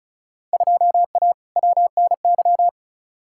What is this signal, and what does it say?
Morse code, heard as a keyed tone, about 36 wpm, 700 Hz. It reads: 2A WNY
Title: Field Day Exchanges - 35wpm